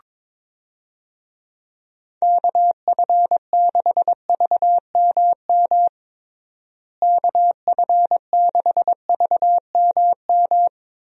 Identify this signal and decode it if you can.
Morse code, heard as a keyed tone, about 22 wpm, 700 Hz. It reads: KF6VMM KF6VMM
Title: USA Call Signs - 6 Characters - Repeated 3x - 22wpm